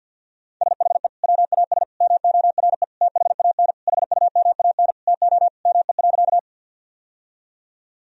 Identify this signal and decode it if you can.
Morse code, heard as a keyed tone, about 50 wpm, 700 Hz. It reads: SHE WAS MORE THAN HUMAN TO ME.
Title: Sentences from Top 1000 Words - Review Part 3 - 50wpm